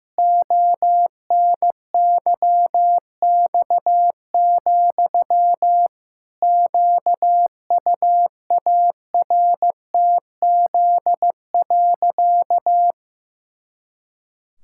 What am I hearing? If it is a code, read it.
Morse code, heard as a keyed tone, about 15 wpm, 700 Hz. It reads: ONYX, QUARTZ.
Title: Pangram Challenge v2 - 15wpm